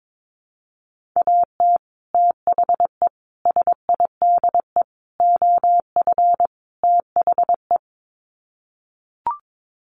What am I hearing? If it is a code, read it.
Morse code, heard as a keyed tone, about 22 wpm, 700 Hz. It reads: AT THE SIDE OF THE
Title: Sets of 5 Words Encore2 22wpm